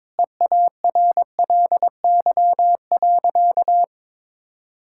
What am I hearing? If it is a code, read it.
Morse code, heard as a keyed tone, about 22 wpm, 700 Hz. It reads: EARLY.
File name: practice / 22wpm / Easy Sentences with Hints - 22wpm